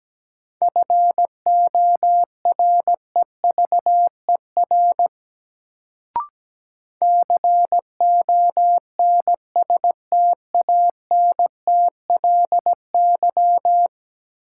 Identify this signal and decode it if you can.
Morse code, heard as a keyed tone, about 17 wpm, 700 Hz. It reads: FOREVER CONSTANTLY
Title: Sentences from Top 1600 Words - New Word Drill Alt - 17wpm